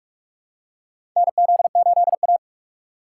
Morse code, heard as a keyed tone, about 46 wpm, 700 Hz. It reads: NZ8A
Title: USA Call Signs - 4 Characters - 6x Character Spacing - 45wpm